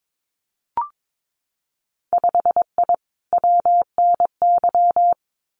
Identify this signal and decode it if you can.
Morse code, heard as a keyed tone, about 22 wpm, 700 Hz. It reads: 5I WNY
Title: Winter Field Day Exchanges Encore 22wpm